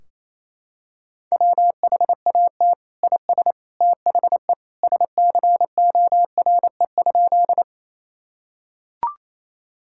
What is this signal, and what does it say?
Morse code, heard as a keyed tone, about 28 wpm, 700 Hz. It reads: WHAT IS THE SCORE?
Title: Sentences from Top 1600 Words 28wpm